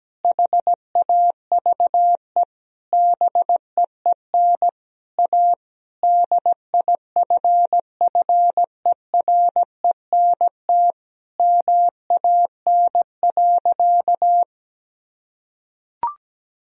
Morse code, heard as a keyed tone, about 17 wpm, 700 Hz. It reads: HAVE BEEN A DIFFERENT MAN.
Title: Sentences from Top 1000 Words - Review Part 3 - 17wpm